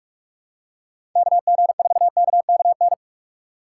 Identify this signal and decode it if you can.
Morse code, heard as a keyed tone, about 45 wpm, 700 Hz. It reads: KG4KKN
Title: USA Call Signs - 6 Characters - 3x Character Spacing - 45wpm